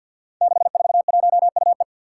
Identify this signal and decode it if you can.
Morse code, heard as a keyed tone, about 47 wpm, 700 Hz. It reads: 641UE
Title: Five Challenging Letter-Number Combinations 50wpm